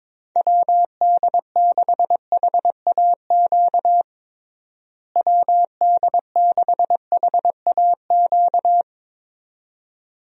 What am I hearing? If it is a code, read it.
Morse code, heard as a keyed tone, about 22 wpm, 700 Hz. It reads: WD6HAQ WD6HAQ
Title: USA Call Signs - 6 Characters - Repeated 2x - 22wpm